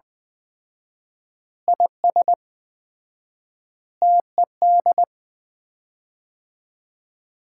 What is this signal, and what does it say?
Morse code, heard as a keyed tone, about 20 wpm, 700 Hz. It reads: IS TED
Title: Common QSO Phrases - 4x Word-Spacing - 20wpm